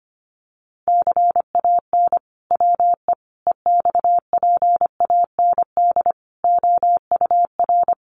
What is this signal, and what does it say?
Morse code, heard as a keyed tone, about 25 wpm, 700 Hz. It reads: CAN WE EXPAND OUR
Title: Sentences from Top 1400 Words 25wpm